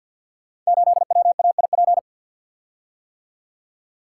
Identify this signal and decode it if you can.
Morse code, heard as a keyed tone, about 47 wpm, 700 Hz. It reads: 8WAIP
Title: Code Groups - 5 Letter:Numbers - 50wpm